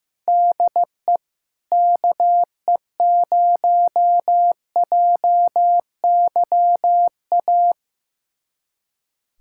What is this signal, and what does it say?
Morse code, heard as a keyed tone, about 15 wpm, 700 Hz. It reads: DE KE0JYA
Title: USA Callsign DE Callsign 15wpm